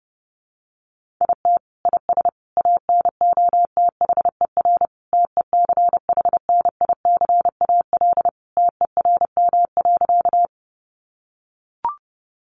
Morse code, heard as a keyed tone, about 30 wpm, 700 Hz. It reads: IT IS ANOTHER TECHNICAL TERM.
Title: Sentences from Top 1100 Words 30wpm